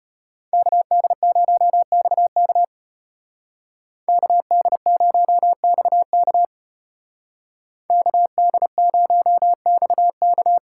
Morse code, heard as a keyed tone, about 35 wpm, 700 Hz. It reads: KD0XK KD0XK KD0XK